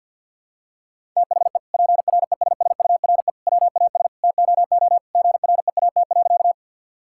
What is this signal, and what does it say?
Morse code, heard as a keyed tone, about 49 wpm, 700 Hz. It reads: THE PRESSURE WAS TOO GREAT.
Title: Sentences from Top 1000 Words - Review Part 6 - 50wpm